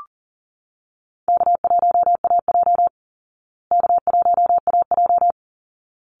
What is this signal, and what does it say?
Morse code, heard as a keyed tone, about 39 wpm, 700 Hz. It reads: K1AJ K1AJ
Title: Easy Call Signs Repeated 2x - 40wpm